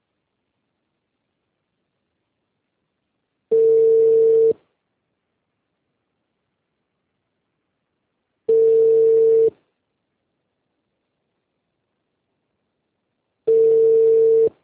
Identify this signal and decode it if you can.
Ringback tone (China, 450 Hz).
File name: out- -2021- - - .18704